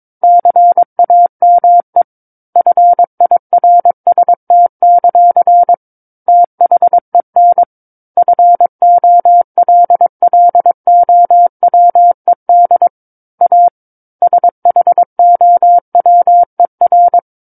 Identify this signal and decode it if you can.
Morse code, heard as a keyed tone, about 22 wpm, 700 Hz. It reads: CAME FIRST; THEN FOLLOWED A SHOWER
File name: 06-Chapter-06 22 WPM